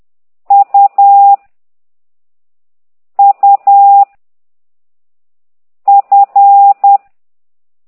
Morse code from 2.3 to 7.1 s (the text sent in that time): U F